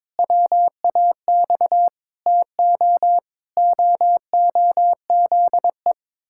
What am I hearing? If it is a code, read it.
Morse code, heard as a keyed tone, about 22 wpm, 700 Hz. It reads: WAX TO OOZE
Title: Pangram Challenge v2 - 22wpm